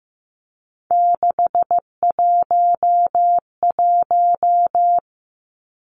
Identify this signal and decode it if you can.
Morse code, heard as a keyed tone, about 15 wpm, 700 Hz. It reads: 611
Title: Numbers - 3-Digit 15wpm